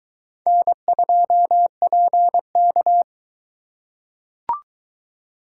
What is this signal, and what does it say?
Morse code, heard as a keyed tone, about 22 wpm, 700 Hz. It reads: N2PK